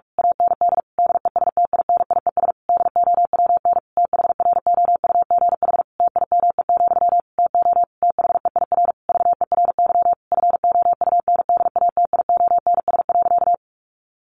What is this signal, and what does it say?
Morse code, heard as a keyed tone, about 44 wpm, 700 Hz. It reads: AND DESTINIES DOWN THROUGH TIME, TO THEIR VERY FOUNDATIONS.
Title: A Sound Of Thunder 45wpm